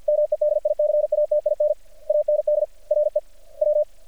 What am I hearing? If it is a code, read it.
Morse code, heard as a keyed tone, about 50 wpm, 600 Hz. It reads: MEDICATIN AND RE W
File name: archived / CRSnet